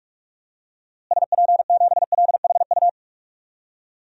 Morse code, heard as a keyed tone, about 42 wpm, 700 Hz. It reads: SP7LHU
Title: DXCC-100 Call Signs - 6 Characters - Repeated 3x - 45wpm